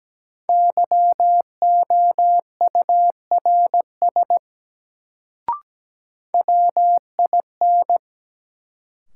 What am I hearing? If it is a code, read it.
Morse code, heard as a keyed tone, about 17 wpm, 700 Hz. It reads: YOURS WIN
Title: Sentences from Top 1500 Words - New Word Drill Alt - 17wpm